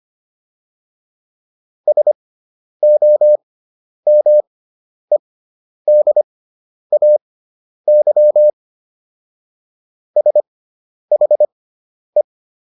Morse code, heard as a keyed tone, about 25 wpm, spaced as for 10 wpm, 600 Hz. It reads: SOMEDAY SHE